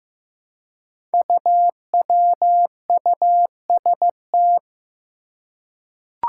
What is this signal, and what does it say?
Morse code, heard as a keyed tone, about 15 wpm, 700 Hz. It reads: UWUST
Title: Five Challenging Letter-Number Combinations 15wpm